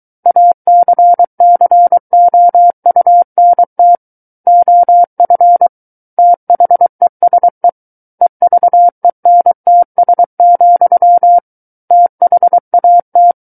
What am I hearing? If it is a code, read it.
Morse code, heard as a keyed tone, about 23 wpm, 700 Hz. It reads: ACCOUNT OF THESE EVENTS, THAT
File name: Book One / 14-Chapter 14 - 23 WPM